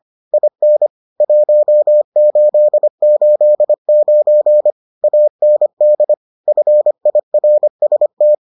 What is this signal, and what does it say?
Morse code, heard as a keyed tone, about 25 wpm, 600 Hz. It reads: IN 1889 AND FIRST